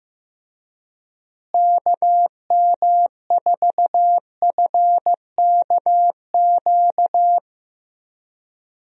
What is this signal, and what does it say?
Morse code, heard as a keyed tone, about 15 wpm, 700 Hz. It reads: KM4FKQ
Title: USA Call Signs - 6 Characters - Repeated 2x - 15wpm